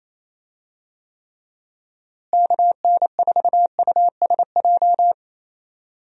Morse code, heard as a keyed tone, about 28 wpm, 700 Hz. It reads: KN4USJ
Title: USA Call Signs - 6 Characters - Repeated 3x - 28wpm